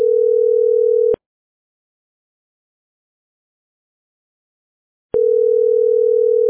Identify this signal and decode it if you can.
Ringback tone (North America), 440 + 480 Hz.